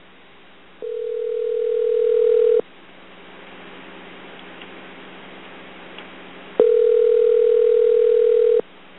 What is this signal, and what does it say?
Ringback tone (North America), 440 + 480 Hz.